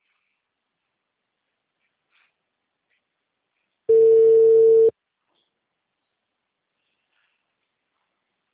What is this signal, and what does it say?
Ringback tone (China, 450 Hz).
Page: Auditoria